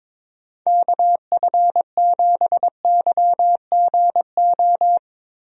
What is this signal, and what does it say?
Morse code, heard as a keyed tone, about 21 wpm, 700 Hz. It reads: KF7YGO
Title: USA Call Signs - 6 Characters - Repeated 2x - 22wpm